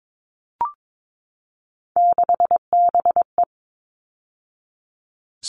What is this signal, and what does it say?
Morse code, heard as a keyed tone, about 22 wpm, 700 Hz. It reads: 6BE
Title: All Cut Numbers - 3-Digit 22wpm